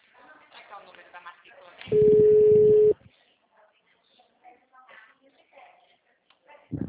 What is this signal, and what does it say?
Ringback tone (Europe, 425 Hz), with speech.
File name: out- -2016- - - .74573